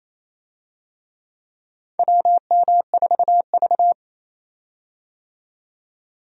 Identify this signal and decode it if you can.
Morse code, heard as a keyed tone, about 27 wpm, 700 Hz. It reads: WM4V
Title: USA Call Signs - 4 Characters - 28wpm